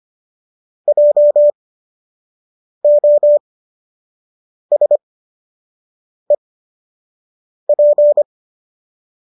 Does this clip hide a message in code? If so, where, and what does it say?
Yes: Morse code from 0.9 to 8.2 s, heard as a keyed tone, about 25 wpm, spaced as for 6 wpm, 590 Hz. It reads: JOSEP